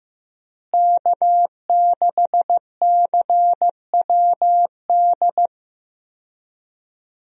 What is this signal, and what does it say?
Morse code, heard as a keyed tone, about 15 wpm, 700 Hz. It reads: K6CWD
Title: USA Call Signs - 5 Characters - Repeated 2x - 15wpm